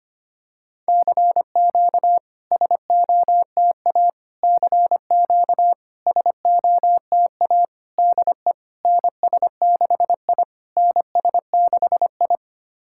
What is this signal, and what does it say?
Morse code, heard as a keyed tone, about 25 wpm, 700 Hz. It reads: CQ SOTA CQ SOTA DE NS6I NS6I
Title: SOTA QSO Practice 25wpm